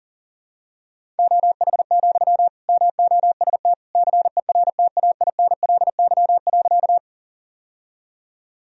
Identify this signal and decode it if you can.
Morse code, heard as a keyed tone, about 40 wpm, 700 Hz. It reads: OH, MOST CERTAINLY.
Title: Sentences from Top 1000 Words - Review Part 1 - 40wpm